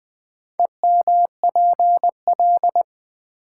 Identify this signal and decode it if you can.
Morse code, heard as a keyed tone, about 20 wpm, 700 Hz. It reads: EMPL
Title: Build A Word - 2 Letters At A Time - 8-Letter Words - 20wpm